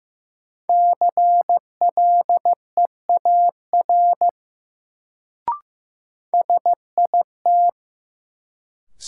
Morse code, heard as a keyed tone, about 15 wpm, 700 Hz. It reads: CLEAR SIT
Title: Sentences from Top 500 Words - New Word Drill - 15wpm